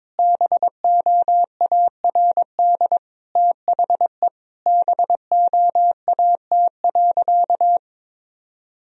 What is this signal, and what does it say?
Morse code, heard as a keyed tone, about 22 wpm, 700 Hz. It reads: BOARD THE BOAT.
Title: Sentences from Top 1000 Words - Review Part 4 - 22wpm